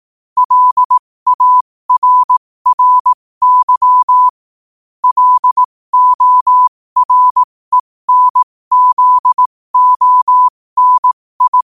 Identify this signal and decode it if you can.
Morse code, heard as a keyed tone, about 18 wpm, spaced as for 15 wpm, 1000 Hz. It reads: LARRY LORENZONI
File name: qotdfiles / 15wpm